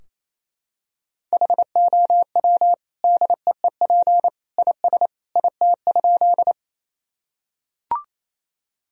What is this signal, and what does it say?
Morse code, heard as a keyed tone, about 28 wpm, 700 Hz. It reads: HOW DEEP IS IT?
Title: Sentences from Top 1000 Words - Review Part 3 - 28wpm